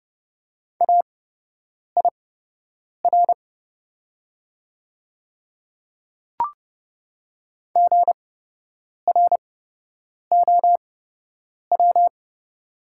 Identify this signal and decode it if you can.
Morse code, heard as a keyed tone, about 30 wpm, spaced as for 8 wpm, 690 Hz. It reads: AIR GROW
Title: 8x Character Spacing - Top 500 Words - 30wpm